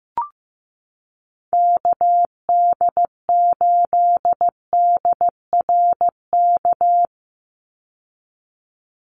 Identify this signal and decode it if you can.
Morse code, heard as a keyed tone, about 15 wpm, 700 Hz. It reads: KD8DRK
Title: USA Call Signs 15wpm